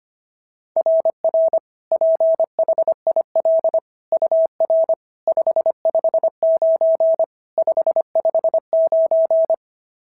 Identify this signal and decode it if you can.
Morse code, heard as a keyed tone, about 25 wpm, 690 Hz. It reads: RR PHIL UR 559 559
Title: SOTA QSO Practice 25wpm